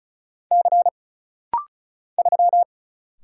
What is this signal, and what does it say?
Morse code, heard as a keyed tone, about 36 wpm, 700 Hz. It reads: C 3
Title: Single Letter-Number Alt - Rapid-Fire 35wpm